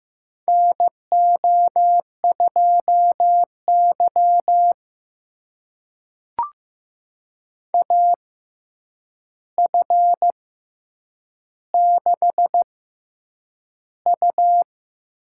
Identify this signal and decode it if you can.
Morse code, heard as a keyed tone, about 15 wpm, spaced as for 8 wpm, 700 Hz. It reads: NO2Y AF6U
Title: USA Call Signs - 4 Characters - 6x Character Spacing - 15wpm